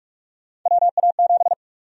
Morse code, heard as a keyed tone, about 42 wpm, 700 Hz. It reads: WA7
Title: USA Call Sign Prefixes - Repeated 2x - 45wpm